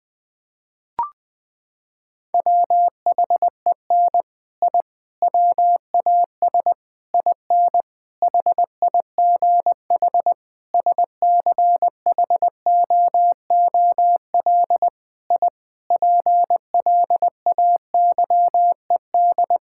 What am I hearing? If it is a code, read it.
Morse code, heard as a keyed tone, about 21 wpm, 700 Hz. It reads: WHEN I WAS IN HIGH SCHOOL I PLAYED
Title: Sentences with 6-Word Phrases - 20wpm